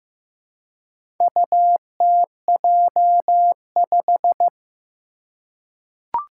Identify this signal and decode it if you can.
Morse code, heard as a keyed tone, about 15 wpm, 700 Hz. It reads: UTJ5
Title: Four Challenging Letter-Number Combinations 15wpm